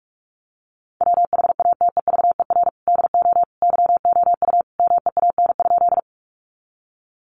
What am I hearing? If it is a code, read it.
Morse code, heard as a keyed tone, about 44 wpm, 700 Hz. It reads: WHATEVER DO YOU MEAN?
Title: Sentences from Top 1100 Words 45wpm